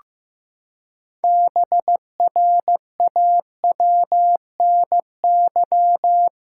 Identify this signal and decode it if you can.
Morse code, heard as a keyed tone, about 15 wpm, 700 Hz. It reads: BRAWNY
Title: Pangram Challenge v2 - 15wpm